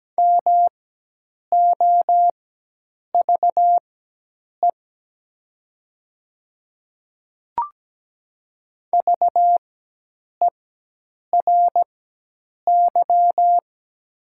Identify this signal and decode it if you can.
Morse code, heard as a keyed tone, about 17 wpm, spaced as for 8 wpm, 700 Hz. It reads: MOVE VERY
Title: 4x Character Spacing - Top 200 Words - 17wpm